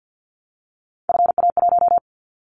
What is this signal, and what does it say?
Morse code, heard as a keyed tone, about 50 wpm, 700 Hz. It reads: FA1
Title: Callsign Prefixes for Top 50 DXCC - 50wpm